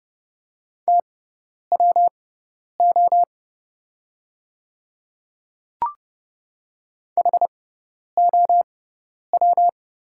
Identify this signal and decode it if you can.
Morse code, heard as a keyed tone, about 30 wpm, spaced as for 10 wpm, 700 Hz. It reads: TWO HOW
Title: 6x Character Spacing - Top 100 Words - 30wpm